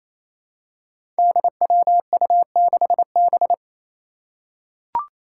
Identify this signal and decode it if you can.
Morse code, heard as a keyed tone, about 28 wpm, 700 Hz. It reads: DWU6B
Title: Five Challenging Letter-Number Combinations 28wpm